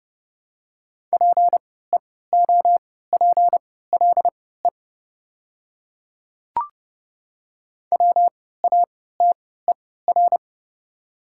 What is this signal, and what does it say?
Morse code, heard as a keyed tone, about 30 wpm, spaced as for 17 wpm, 700 Hz. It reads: PEOPLE WATER
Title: 3x Character Spacing - Top 300 Words - 30wpm